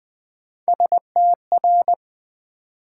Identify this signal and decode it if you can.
Morse code, heard as a keyed tone, about 20 wpm, 690 Hz. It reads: STR
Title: Build A Word - 3 Letters At A Time - 6-Letter Words - 20wpm